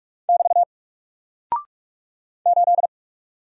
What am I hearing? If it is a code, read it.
Morse code, heard as a keyed tone, about 44 wpm, 700 Hz. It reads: <BT> 8
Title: Single Character (Letter:Number:Punctuation:Prosign) 45wpm